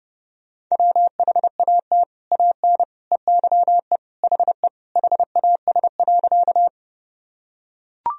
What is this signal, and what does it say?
Morse code, heard as a keyed tone, about 30 wpm, 700 Hz. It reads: WHAT AN EYE HE HAS.
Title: Sentences from Top 1000 Words - Review Part 1 - 30wpm